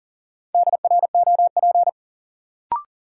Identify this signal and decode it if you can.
Morse code, heard as a keyed tone, about 40 wpm, 700 Hz. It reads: DROP
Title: Sentences from Top 1400 Words - New Word Drill - 40wpm